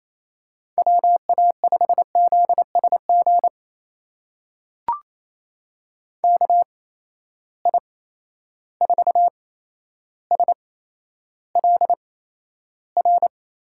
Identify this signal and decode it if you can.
Morse code, heard as a keyed tone, about 28 wpm, spaced as for 12 wpm, 700 Hz. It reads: WA5ZSG KI4SLR
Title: USA Call Signs - 6 Characters - 8x Character Spacing - 28wpm